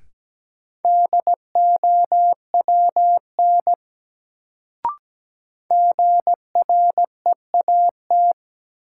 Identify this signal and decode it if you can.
Morse code, heard as a keyed tone, about 17 wpm, 700 Hz. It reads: DOWN GREAT
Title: Sets of 2 Words - New Word Drill - 17wpm